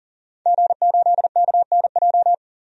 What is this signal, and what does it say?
Morse code, heard as a keyed tone, about 40 wpm, 700 Hz. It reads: G8KNJ
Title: DXCC-100 Call Signs - 5 Characters - 40wpm